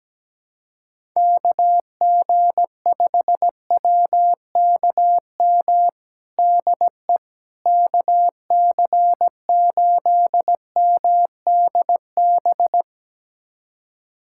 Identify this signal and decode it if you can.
Morse code, heard as a keyed tone, about 17 wpm, 700 Hz. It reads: KG5WKM DE KC8MDB
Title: USA Callsign DE Callsign 17wpm